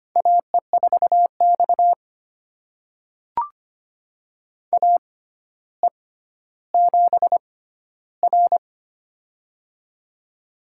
Morse code, heard as a keyed tone, about 25 wpm, spaced as for 13 wpm, 700 Hz. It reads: AE4X AE7R
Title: USA Call Signs - 4 Characters - 6x Character Spacing - 25wpm